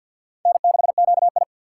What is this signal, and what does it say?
Morse code, heard as a keyed tone, about 52 wpm, 700 Hz. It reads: N6QI
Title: USA Call Signs - 4 Characters - 2x Character Spacing - 50wpm